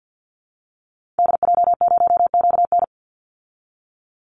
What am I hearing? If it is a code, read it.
Morse code, heard as a keyed tone, about 52 wpm, 700 Hz. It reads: DJ0QN
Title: Call Signs Repeated 2x - 50wpm